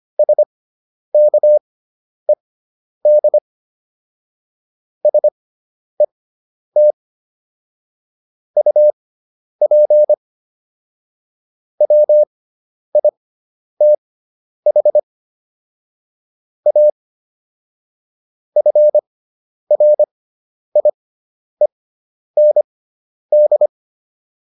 Morse code, heard as a keyed tone, about 25 wpm, spaced as for 10 wpm, 600 Hz. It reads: SKED SET UP WITH A FRIEND